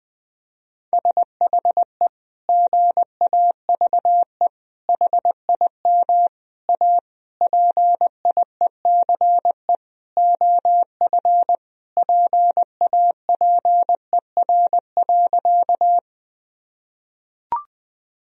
Morse code, heard as a keyed tone, about 21 wpm, 700 Hz. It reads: SHE GAVE HIM A PIECE OF PAPER.
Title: Sentences from Top 1000 Words - Review Part 2 - 20wpm